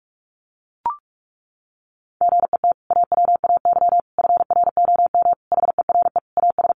Morse code, heard as a keyed tone, about 44 wpm, 700 Hz. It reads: GET AWAY FROM HERE AS
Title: Sentences from Top 600 Words 45wpm